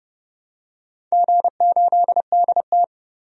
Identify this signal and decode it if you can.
Morse code, heard as a keyed tone, about 30 wpm, 700 Hz. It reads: G8DT
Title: DXCC-100 Call Signs - 4 Characters - 30wpm